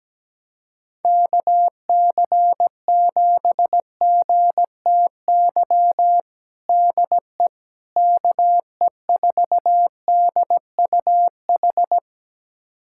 Morse code, heard as a keyed tone, about 17 wpm, 700 Hz. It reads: KC7GTY DE KE4DUH
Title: USA Callsign DE Callsign - 6 Characters - 17wpm